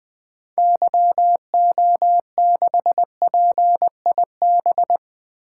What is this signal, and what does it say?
Morse code, heard as a keyed tone, about 20 wpm, 700 Hz. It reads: YO6PIB
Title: 2023 SOTA Activator Call Signs - 20wpm